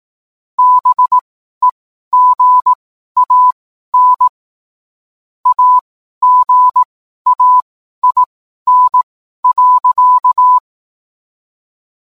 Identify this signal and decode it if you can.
Morse code, heard as a keyed tone, about 18 wpm, spaced as for 12 wpm, 1000 Hz. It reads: BEGAN AGAIN.